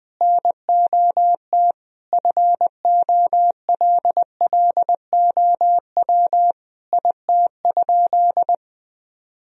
Morse code, heard as a keyed tone, about 20 wpm, 700 Hz. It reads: NOT FOLLOW IT?
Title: Sentences from Top 1000 Words - Review Part 1 - 20wpm